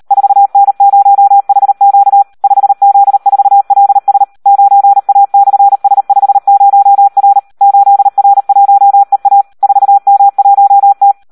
Morse code, heard as a keyed tone, about 38 wpm, 800 Hz. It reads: 4N0HQ 5Z4LS 9A/S50R 8R1EA 4M1T
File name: Audio practica CW / SD 1 38